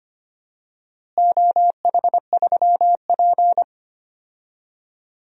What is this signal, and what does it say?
Morse code, heard as a keyed tone, about 25 wpm, 700 Hz. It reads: OH3P
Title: DXCC-100 Call Signs - 4 Characters - Repeated 2x - 25wpm